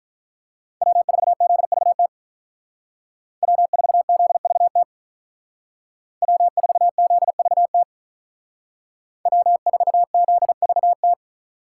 Morse code, heard as a keyed tone, about 42 wpm, 700 Hz. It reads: W4ZVT W4ZVT W4ZVT W4ZVT
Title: Speed Racing - USA 1x3 Call Signs 35wpm